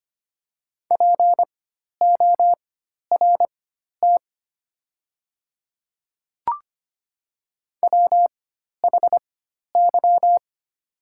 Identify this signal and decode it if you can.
Morse code, heard as a keyed tone, about 25 wpm, spaced as for 12 wpm, 700 Hz. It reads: PORT WHY
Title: 4x Character Spacing - Top 200 Words - 25wpm